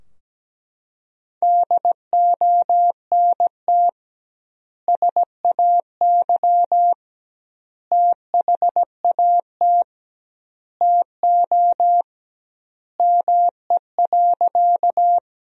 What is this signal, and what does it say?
Morse code, heard as a keyed tone, about 17 wpm, 700 Hz. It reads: DONT SAY THAT TO ME.
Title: 2x Word-Spacing - Sentences from Top 300 Words - 17wpm